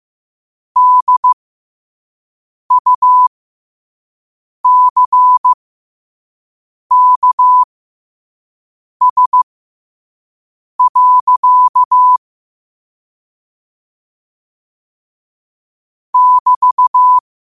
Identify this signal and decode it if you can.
Morse code, heard as a keyed tone, about 15 wpm, spaced as for 5 wpm, 1000 Hz. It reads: DUCKS. <BT>
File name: qotdfiles / 5wpm